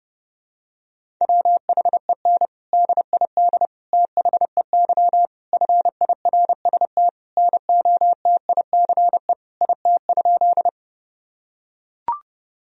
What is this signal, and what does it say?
Morse code, heard as a keyed tone, about 32 wpm, 700 Hz. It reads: WHEN DID THEY FIRST NOTICE IT?
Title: Sentences from Top 1000 Words - Review Part 2 - 30wpm